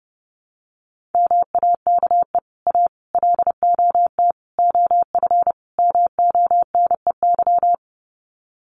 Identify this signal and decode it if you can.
Morse code, heard as a keyed tone, about 30 wpm, 700 Hz. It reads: MAKE A LOT OF MONEY
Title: Sets of 5 Words Encore2 30wpm